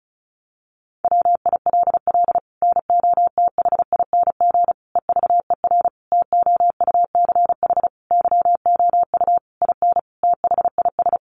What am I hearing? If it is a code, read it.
Morse code, heard as a keyed tone, about 34 wpm, 700 Hz. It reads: WILL NOTHING EVER TOUCH YOU IN THIS
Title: Sentences from Top 1200 Words 35wpm